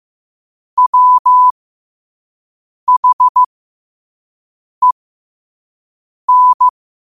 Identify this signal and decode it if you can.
Morse code, heard as a keyed tone, about 15 wpm, spaced as for 5 wpm, 1000 Hz. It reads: WHEN
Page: Quotes for Wed, 13 Aug 2025 in Morse Code at 5 words per minute.